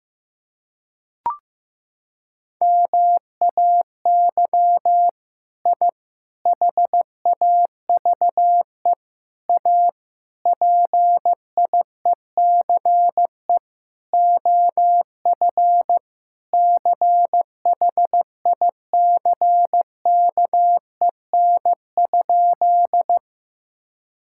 Morse code, heard as a keyed tone, about 15 wpm, 700 Hz. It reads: MAY I HAVE A PIECE OF CHICKEN?
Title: Sentences from Top 1000 Words - Review Part 5 - 15wpm